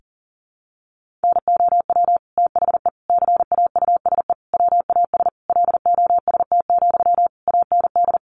Morse code, heard as a keyed tone, about 40 wpm, 700 Hz. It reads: NOW THE CAUSE WAS LOST, AND
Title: Sentences from Top 1500 Words 40wpm